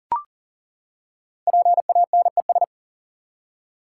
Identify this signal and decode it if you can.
Morse code, heard as a keyed tone, about 39 wpm, 700 Hz. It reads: PANES
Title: Code Groups - 5 Letters - 40wpm